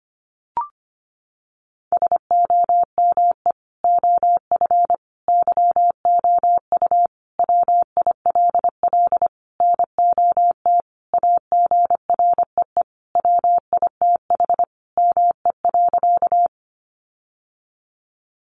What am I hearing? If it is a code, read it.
Morse code, heard as a keyed tone, about 25 wpm, 700 Hz. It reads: SOME OF YOU WILL NOT AGREE WITH ME.
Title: Sentences from Top 1100 Words 25wpm